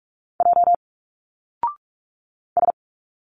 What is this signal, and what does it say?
Morse code, heard as a keyed tone, about 46 wpm, 700 Hz. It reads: J S
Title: Single Letters 45wpm